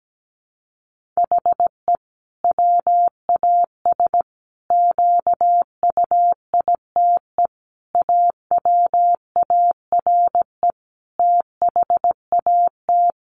Morse code, heard as a keyed tone, about 17 wpm, 700 Hz. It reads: HE WAS QUITE AWARE THAT
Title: Sentences from Top 1400 Words 17wpm